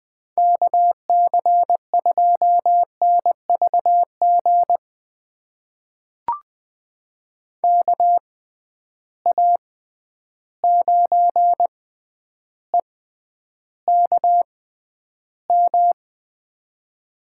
Morse code, heard as a keyed tone, about 20 wpm, spaced as for 10 wpm, 700 Hz. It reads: KC2NVG KA9EKM
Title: USA Call Signs - 6 Characters - 6x Character Spacing - 20wpm